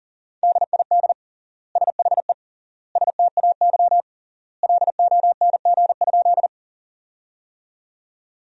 Morse code, heard as a keyed tone, about 40 wpm, 700 Hz. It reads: DID SHE STAY LONG?
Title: 3x Word-Spacing - Sentences from Top 500 Words - 40wpm